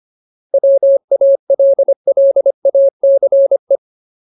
Morse code, heard as a keyed tone, about 25 wpm, 550 Hz. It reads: WALLACE
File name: Ultimate Word List / 25WPM Ultimate